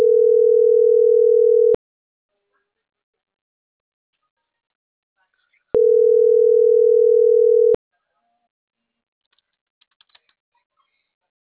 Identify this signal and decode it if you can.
Ringback tone (North America), 440 + 480 Hz.